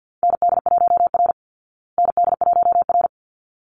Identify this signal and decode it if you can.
Morse code, heard as a keyed tone, about 48 wpm, 700 Hz. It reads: ND1R ND1R
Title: Call Signs Repeated 3x - 50wpm